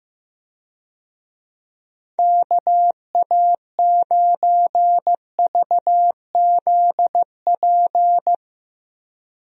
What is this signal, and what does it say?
Morse code, heard as a keyed tone, about 15 wpm, 700 Hz. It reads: KA9VZP
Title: USA Call Signs - 6 Characters - 15wpm